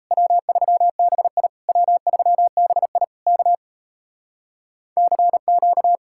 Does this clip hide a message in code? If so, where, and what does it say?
Yes: Morse code from 0.1 to 6.0 s, heard as a keyed tone, about 37 wpm, 700 Hz. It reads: W3BI W3BI K CQ